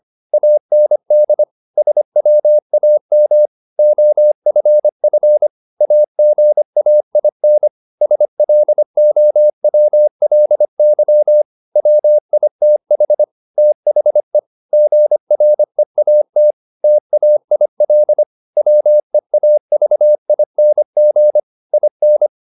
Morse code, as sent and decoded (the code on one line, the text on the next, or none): .- -. -.. / ... .-- .- -- / --- ..-. ..-. / .- --. .- .. -. / ... .-.. --- .-- .-.. -.-- / .-- .. - .... / - .... / --. .-. .- - / - .- .. .-.. / .-- . .- ...- .. -. --. / .. -.
AND SWAM OFF AGAIN SLOWLY WITH THE GREAT TAIL WEAVING IN